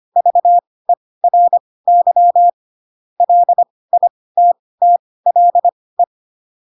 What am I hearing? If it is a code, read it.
Morse code, heard as a keyed tone, about 25 wpm, spaced as for 18 wpm, 700 Hz. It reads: VERY LITTLE
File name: Pinocchio / Pinocchio ch4 12 25 ik1hge